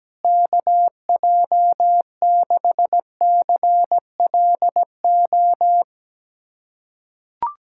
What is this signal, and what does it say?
Morse code, heard as a keyed tone, about 17 wpm, 700 Hz. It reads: KJ6CLO
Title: USA Callsign DE Callsign - 6 Characters - 17wpm